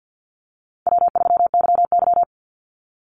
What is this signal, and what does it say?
Morse code, heard as a keyed tone, about 48 wpm, 700 Hz. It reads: W3YY
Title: Easy Call Signs - 50wpm